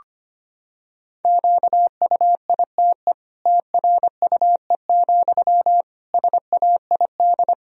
Morse code, heard as a keyed tone, about 25 wpm, 700 Hz. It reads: QUITE TRUE, SAID
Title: Sentences from Top 1000 Words - Review Part 3 - 25wpm